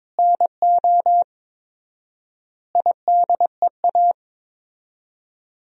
Morse code, heard as a keyed tone, about 21 wpm, 700 Hz. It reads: NO IDEA
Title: 4x Word-Spacing - Sentences from Top 300 Words - 22wpm